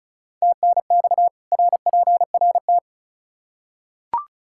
Morse code, heard as a keyed tone, about 36 wpm, 700 Hz. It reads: TNX RPRT
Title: Practice QSO - Exchange 2 - Info - 35wpm